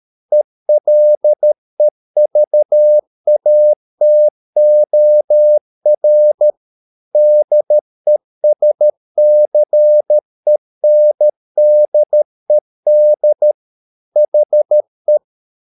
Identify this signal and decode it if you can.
Morse code, heard as a keyed tone, about 13 wpm, 600 Hz. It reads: ELEVATOR DESCENDED HE